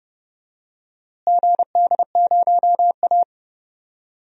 Morse code, heard as a keyed tone, about 32 wpm, 700 Hz. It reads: GD0A
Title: DXCC-100 Call Signs - 4 Characters - Repeated 3x - 30wpm